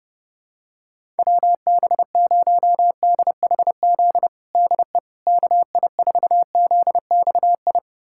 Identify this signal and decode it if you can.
Morse code, heard as a keyed tone, about 29 wpm, 700 Hz. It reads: WB0DHZ DE KI4ZXI
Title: USA Callsign DE Callsign - 6 Characters - 30wpm